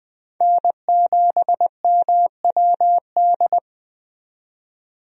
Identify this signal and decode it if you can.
Morse code, heard as a keyed tone, about 20 wpm, 700 Hz. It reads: N7MWD
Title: USA Call Signs - 5 Characters - 20wpm